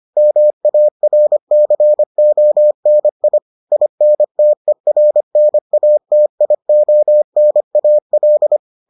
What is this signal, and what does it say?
Morse code, heard as a keyed tone, about 25 wpm, 600 Hz. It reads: MARCONI INTERNATIONAL